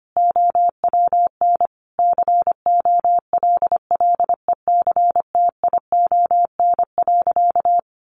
Morse code, heard as a keyed tone, about 25 wpm, 700 Hz. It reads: OWN COLLECTION.
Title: Sentences from Top 1200 Words 25wpm